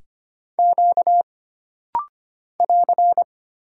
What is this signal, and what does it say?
Morse code, heard as a keyed tone, about 25 wpm, 700 Hz. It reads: Q <AR>
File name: Single Character (Letter:Number:Punctuation:Prosign) - Rapid-Fire - 25wpm